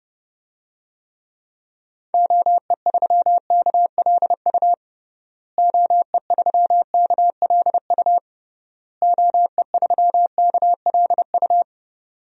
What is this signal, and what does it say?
Morse code, heard as a keyed tone, about 29 wpm, 700 Hz. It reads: OE3KLU OE3KLU OE3KLU
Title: DXCC-100 Call Signs - 6 Characters - Repeated 3x - 30wpm